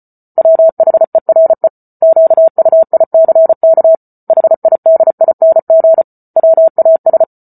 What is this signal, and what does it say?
Morse code, heard as a keyed tone, about 34 wpm, 650 Hz. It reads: WHERE QUICK HIDING WAS